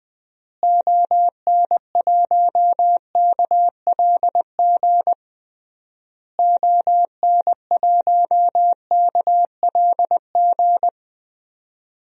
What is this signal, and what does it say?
Morse code, heard as a keyed tone, about 20 wpm, 700 Hz. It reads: ON1KLG ON1KLG
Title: DXCC-100 Call Signs - 6 Characters - Repeated 3x - 20wpm